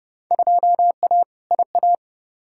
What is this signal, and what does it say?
Morse code, heard as a keyed tone, about 30 wpm, 700 Hz. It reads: 2A IA
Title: Field Day Exchanges - 30wpm